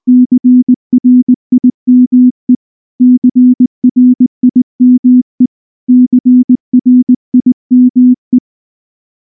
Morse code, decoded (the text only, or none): CRIME CRIME CRIME